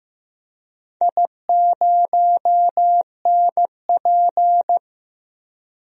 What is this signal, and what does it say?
Morse code, heard as a keyed tone, about 15 wpm, 700 Hz. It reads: I0NP
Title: DXCC-100 Call Signs - 4 Characters - Repeated 3x - 15wpm